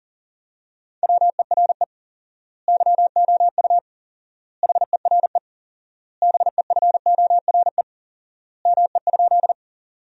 Morse code, heard as a keyed tone, about 40 wpm, 700 Hz. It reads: WERE YOU HERE BEFORE ME?
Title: 4x Word-Spacing - Sentences from Top 200 Words - 40wpm